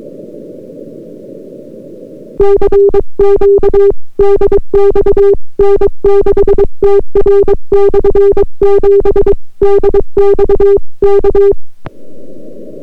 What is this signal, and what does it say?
Morse code, heard as a keyed tone, about 22 wpm, 400 Hz. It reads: CQ DX N6TR/7 DX K